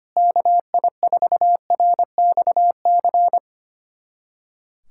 Morse code, heard as a keyed tone, about 25 wpm, 700 Hz. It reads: KI4RXC
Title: USA Callsign DE Callsign 25wpm